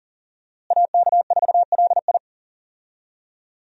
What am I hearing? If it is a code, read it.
Morse code, heard as a keyed tone, about 39 wpm, 700 Hz. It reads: AK4LI
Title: USA Call Signs - 5 Characters - Repeated 3x - 40wpm